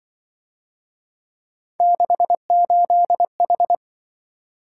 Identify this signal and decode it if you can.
Morse code, heard as a keyed tone, about 25 wpm, 700 Hz. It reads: 68H
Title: Speed-Racing - Three Letter-Number Combinations - 20wpm